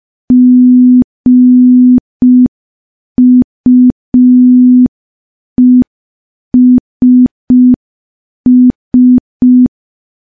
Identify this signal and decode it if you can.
Morse code, heard as a keyed tone, about 5 wpm, 250 Hz. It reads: GUESS
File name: morse / morse-training-05WPM